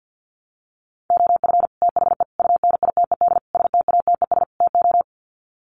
Morse code, heard as a keyed tone, about 50 wpm, 700 Hz. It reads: OF THE UNITED STATES TO
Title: Sets of 5 Words Encore 50wpm